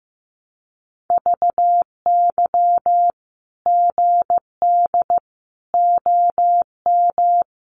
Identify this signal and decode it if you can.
Morse code, heard as a keyed tone, about 15 wpm, 700 Hz. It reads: VY GD OM
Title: Common QSO Phrases 15wpm